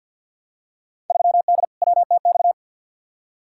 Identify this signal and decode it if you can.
Morse code, heard as a keyed tone, about 48 wpm, 700 Hz. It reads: 3D WTX
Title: Field Day Exchanges Encore - 50wpm